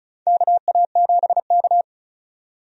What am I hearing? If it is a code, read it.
Morse code, heard as a keyed tone, about 34 wpm, 700 Hz. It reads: KA7K
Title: USA Call Signs - 4 Characters - 35wpm